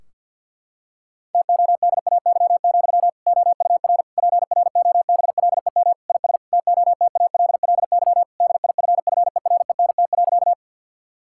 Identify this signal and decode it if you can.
Morse code, heard as a keyed tone, about 50 wpm, 700 Hz. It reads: TODAY, OUR PROBLEM IS TOTALLY DIFFERENT.
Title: Sentences from Top 1600 Words 50wpm